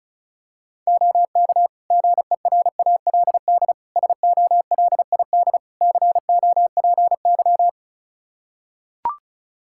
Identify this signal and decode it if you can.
Morse code, heard as a keyed tone, about 34 wpm, 700 Hz. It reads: OK GERALD SOLID COPY
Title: Practice QSO - Exchange 3 - More Info - 35wpm